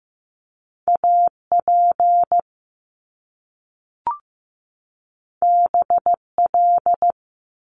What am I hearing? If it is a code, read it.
Morse code, heard as a keyed tone, about 15 wpm, 700 Hz. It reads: AP BL
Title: Top Two Letter Combinations in Words 15wpm